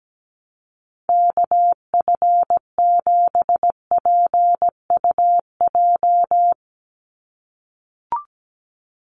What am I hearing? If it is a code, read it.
Morse code, heard as a keyed tone, about 17 wpm, 700 Hz. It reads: KF7PUJ
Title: USA Call Signs 17wpm